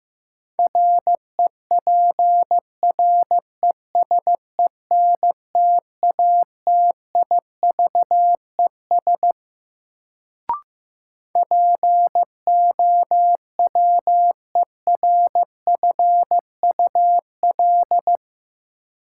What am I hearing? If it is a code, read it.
Morse code, heard as a keyed tone, about 15 wpm, 700 Hz. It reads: REPRESENTATIVES POWERFUL
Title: Sets of 5 Words Encore - New Word Drill - 15wpm